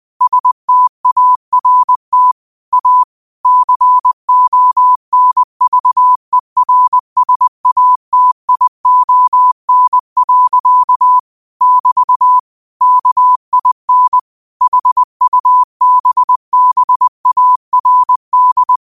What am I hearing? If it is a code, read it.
Morse code, heard as a keyed tone, about 20 wpm, 1000 Hz. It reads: START A CONVERSATION. <BT> KIN HUBBARD
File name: qotdfiles / 20wpm